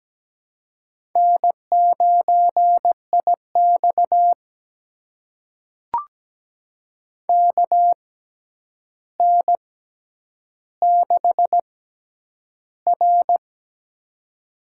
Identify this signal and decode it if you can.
Morse code, heard as a keyed tone, about 17 wpm, spaced as for 9 wpm, 700 Hz. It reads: N9IX KN6R
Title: USA Call Signs - 4 Characters - 6x Character Spacing - 17wpm